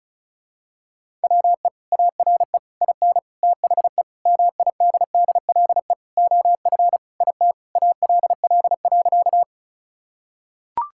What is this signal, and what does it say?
Morse code, heard as a keyed tone, about 36 wpm, 700 Hz. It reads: WE ARE IN THE MIDDLE OF IT ALL.
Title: Sentences from Top 1000 Words - Review Part 6 - 35wpm